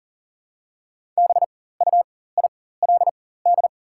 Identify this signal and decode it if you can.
Morse code, heard as a keyed tone, about 40 wpm, spaced as for 19 wpm, 700 Hz. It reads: BUILD
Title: 4x Character Spacing - Top 300 Words - 40wpm